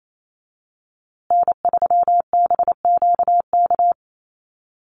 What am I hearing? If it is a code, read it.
Morse code, heard as a keyed tone, about 28 wpm, 700 Hz. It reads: N3BQK
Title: USA Call Signs Repeated 2x 28wpm